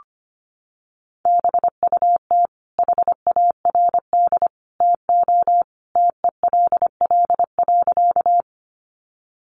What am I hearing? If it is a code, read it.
Morse code, heard as a keyed tone, about 25 wpm, 700 Hz. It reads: BUT HARD TO TELL.
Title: Sentences from Top 300 Words 25wpm